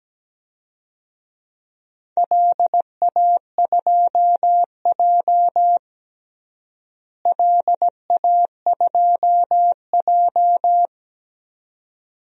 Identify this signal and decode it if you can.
Morse code, heard as a keyed tone, about 17 wpm, 700 Hz. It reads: LA2J LA2J
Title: DXCC-100 Call Signs - 4 Characters - Repeated 2x - 17wpm